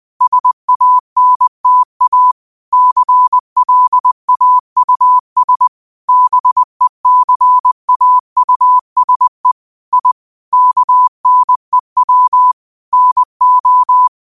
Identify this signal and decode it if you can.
Morse code, heard as a keyed tone, about 20 wpm, 1000 Hz. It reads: SANTA CLAUS BECAUSE I KNEW NO